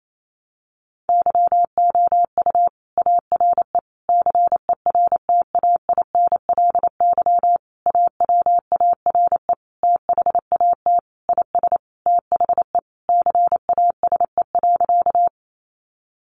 Morse code, heard as a keyed tone, about 28 wpm, 700 Hz. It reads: YOU ARE CERTAINLY AWARE THAT IS THE CASE.
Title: Sentences from Top 1200 Words 28wpm